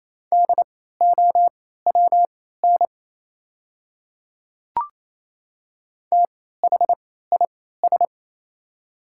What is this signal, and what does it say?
Morse code, heard as a keyed tone, about 28 wpm, spaced as for 16 wpm, 690 Hz. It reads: DOWN THIS
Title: 3x Character Spacing - Top 200 Words - 28wpm